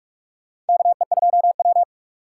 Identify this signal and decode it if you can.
Morse code, heard as a keyed tone, about 46 wpm, 700 Hz. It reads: KE2W
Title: USA Call Signs - 3x Character Spacing - 45wpm